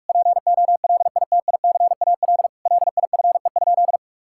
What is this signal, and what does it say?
Morse code, heard as a keyed tone, about 46 wpm, 700 Hz. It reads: POLITICAL LIFE?
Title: Sentences from Top 1000 Words - Review Part 5 - 45wpm